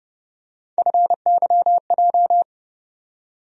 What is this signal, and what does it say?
Morse code, heard as a keyed tone, about 31 wpm, 700 Hz. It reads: FYJ
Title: USA Call Sign Postfixes - 3 Letters 30wpm